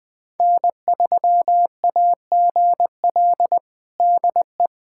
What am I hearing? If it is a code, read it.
Morse code, heard as a keyed tone, about 21 wpm, 700 Hz. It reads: N3AGL DE
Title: USA Callsign DE Callsign - 5 Characters - 20wpm